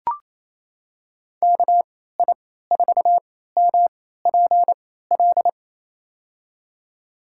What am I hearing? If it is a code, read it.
Morse code, heard as a keyed tone, about 28 wpm, spaced as for 16 wpm, 700 Hz. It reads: KI4MPL
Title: USA Call Signs - 6 Characters - 3x Character Spacing - 28wpm